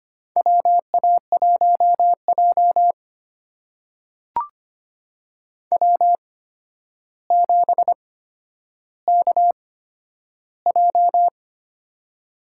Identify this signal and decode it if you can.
Morse code, heard as a keyed tone, about 25 wpm, spaced as for 11 wpm, 700 Hz. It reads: WA1J W7KJ
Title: USA Call Signs - 4 Characters - 8x Character Spacing - 25wpm